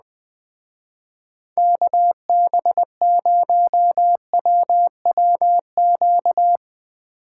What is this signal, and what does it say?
Morse code, heard as a keyed tone, about 20 wpm, 700 Hz. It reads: KB0WWQ
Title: Speed Racing - USA Call Signs 17wpm